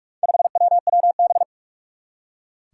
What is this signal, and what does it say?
Morse code, heard as a keyed tone, about 46 wpm, 700 Hz. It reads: 5WWB
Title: Four Challenging Letter-Number Combinations 45wpm